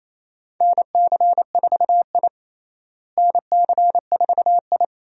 Morse code, heard as a keyed tone, about 28 wpm, 700 Hz. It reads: NC4I NC4I
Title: USA Call Signs - 4 Characters - Repeated 2x - 28wpm